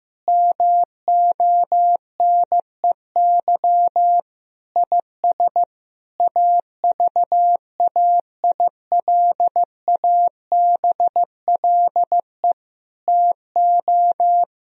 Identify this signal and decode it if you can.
Morse code, heard as a keyed tone, about 15 wpm, 700 Hz. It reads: MONEY IS AVAILABLE TO
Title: Sentences from Top 1000 Words - Review Part 3 - 15wpm